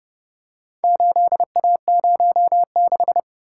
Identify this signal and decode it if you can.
Morse code, heard as a keyed tone, about 31 wpm, 700 Hz. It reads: 8A06